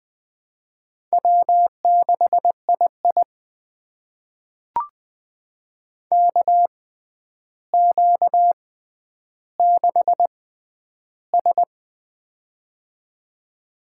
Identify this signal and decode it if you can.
Morse code, heard as a keyed tone, about 20 wpm, spaced as for 10 wpm, 700 Hz. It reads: W6II KQ6S
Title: USA Call Signs - 4 Characters - 6x Character Spacing - 20wpm